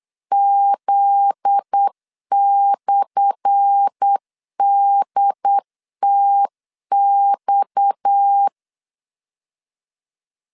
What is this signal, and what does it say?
Morse code, heard as a keyed tone, about 8 wpm, 800 Hz. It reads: Z/DTX
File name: LEZIONI / 10 SIGNS / 10A SLASH 8wpm 3452